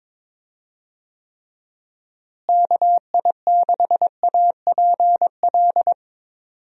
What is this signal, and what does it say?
Morse code, heard as a keyed tone, about 22 wpm, 700 Hz. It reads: KI6APL
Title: USA Call Signs - 6 Characters - Repeated 3x - 22wpm